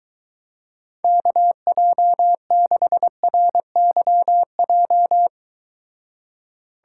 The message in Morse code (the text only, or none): KJ6RYJ